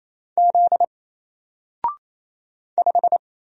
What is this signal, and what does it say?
Morse code, heard as a keyed tone, about 28 wpm, 700 Hz. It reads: Z 5
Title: Single Character (Letter:Number:Punctuation:Prosign) Alt 28wpm